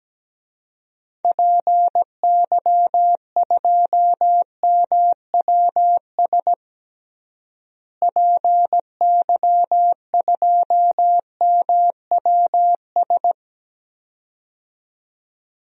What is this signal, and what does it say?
Morse code, heard as a keyed tone, about 17 wpm, 700 Hz. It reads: PY2MWS PY2MWS
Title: DXCC-100 Call Signs - 6 Characters - Repeated 2x - 17wpm